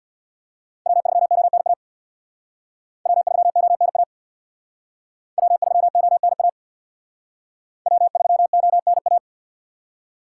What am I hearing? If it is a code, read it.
Morse code, heard as a keyed tone, about 61 wpm, 700 Hz. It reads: W3ONA W3ONA W3ONA W3ONA
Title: Speed Racing - USA 1x3 Call Signs 50wpm